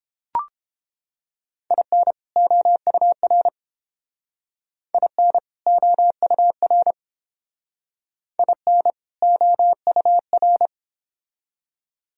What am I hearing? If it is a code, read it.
Morse code, heard as a keyed tone, about 30 wpm, 700 Hz. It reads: IN OUR IN OUR IN OUR